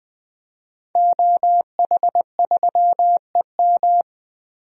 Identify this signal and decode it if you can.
Morse code, heard as a keyed tone, about 21 wpm, 700 Hz. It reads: OH3EM
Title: DXCC-100 Call Signs - 5 Characters - Repeated 3x - 20wpm